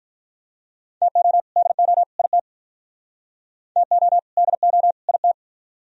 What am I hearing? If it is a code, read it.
Morse code, heard as a keyed tone, about 46 wpm, 700 Hz. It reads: TO DO IT TO DO IT